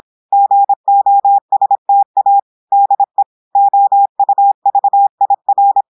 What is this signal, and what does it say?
Morse code, heard as a keyed tone, about 26 wpm, 800 Hz. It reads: GOSTA DE OUVIR
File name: portugais texte en clair 2 vitesse 26 mots minutes